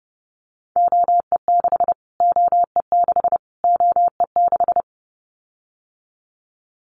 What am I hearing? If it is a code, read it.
Morse code, heard as a keyed tone, about 30 wpm, 700 Hz. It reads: OE6 OE6 OE6
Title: Callsign Prefixes for Top 25 DXCC - Repeated 3x - 30wpm